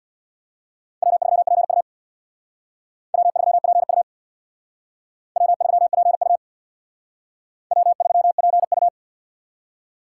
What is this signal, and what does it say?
Morse code, heard as a keyed tone, about 61 wpm, 700 Hz. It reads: W3PU W3PU W3PU W3PU
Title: Speed Racing - USA 1x2 Call Signs 50wpm